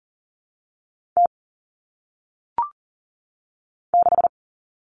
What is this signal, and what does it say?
Morse code, heard as a keyed tone, about 41 wpm, 700 Hz. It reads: T 6
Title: All Cut Numbers - 1-Digit 40wpm